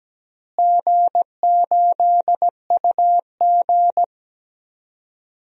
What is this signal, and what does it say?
Morse code, heard as a keyed tone, about 17 wpm, 700 Hz. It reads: G8UG
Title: DXCC-100 Call Signs - 4 Characters - Repeated 2x - 17wpm